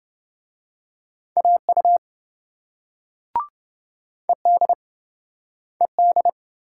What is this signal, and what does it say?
Morse code, heard as a keyed tone, about 27 wpm, 700 Hz. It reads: AU ED ED